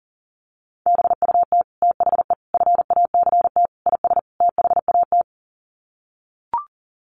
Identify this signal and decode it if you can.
Morse code, heard as a keyed tone, about 40 wpm, 700 Hz. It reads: BUT THE FACT IS THAT
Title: Sets of 5 Words Encore2 40wpm